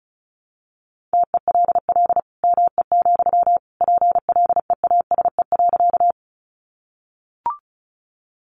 Morse code, heard as a keyed tone, about 34 wpm, 700 Hz. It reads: TELL ME, PLEASE.
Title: Sentences from Top 1500 Words 35wpm